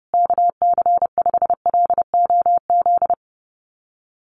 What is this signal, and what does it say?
Morse code, heard as a keyed tone, about 30 wpm, 700 Hz. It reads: KC5LOZ
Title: USA Call Signs Repeated 2x 30wpm